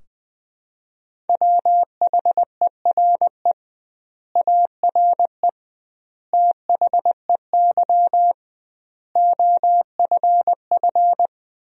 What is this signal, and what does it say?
Morse code, heard as a keyed tone, about 20 wpm, 700 Hz. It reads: WHERE ARE THEY OFF
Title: 2x Word-Spacing - Sentences from Top 200 Words - 20wpm